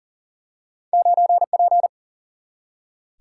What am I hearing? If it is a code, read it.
Morse code, heard as a keyed tone, about 38 wpm, 700 Hz. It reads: 9P
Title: Single-Letter Single-Number 40wpm